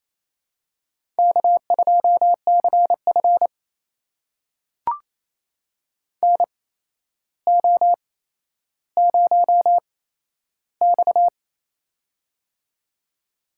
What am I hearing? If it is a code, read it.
Morse code, heard as a keyed tone, about 28 wpm, spaced as for 12 wpm, 700 Hz. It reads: K2CF NO0X
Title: USA Call Signs - 8x Character Spacing - 28wpm